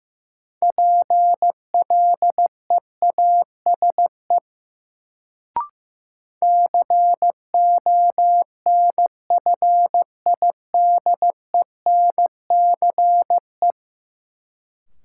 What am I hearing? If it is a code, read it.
Morse code, heard as a keyed tone, about 15 wpm, 700 Hz. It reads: PLEASE CONFIDENCE
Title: Sentences from Top 1500 Words - New Word Drill - 15wpm